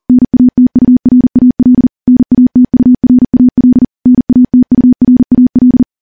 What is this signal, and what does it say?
Morse code, heard as a keyed tone, about 40 wpm, 250 Hz. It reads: NATURAL NATURAL NATURAL
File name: morse-training-40WPM